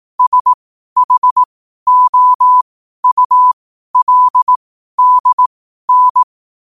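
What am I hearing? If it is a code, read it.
Morse code, heard as a keyed tone, about 18 wpm, spaced as for 12 wpm, 1000 Hz. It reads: SHOULDN
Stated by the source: 12 wpm Morse.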